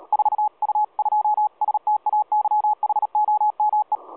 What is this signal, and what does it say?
Morse code, heard as a keyed tone, about 32 wpm, 840 Hz. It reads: 4U2STAYHOME
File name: 4U2STAYHOME 7016.2kHz